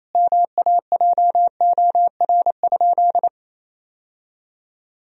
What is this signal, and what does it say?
Morse code, heard as a keyed tone, about 28 wpm, 700 Hz. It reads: MAJOR?
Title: Sentences from Top 1000 Words - Review Part 4 - 28wpm